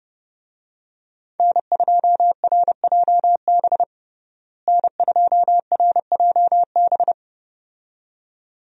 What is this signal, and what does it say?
Morse code, heard as a keyed tone, about 32 wpm, 700 Hz. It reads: N2RJB N2RJB
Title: USA Call Signs - 5 Characters - Repeated 2x - 30wpm